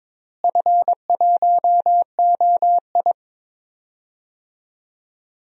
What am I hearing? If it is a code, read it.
Morse code, heard as a keyed tone, about 22 wpm, 700 Hz. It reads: F1OI
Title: DXCC-100 Call Signs - 4 Characters - Repeated 3x - 22wpm